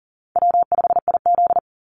Morse code, heard as a keyed tone, about 38 wpm, 700 Hz. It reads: W5IZ
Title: Easy Call Signs - 40wpm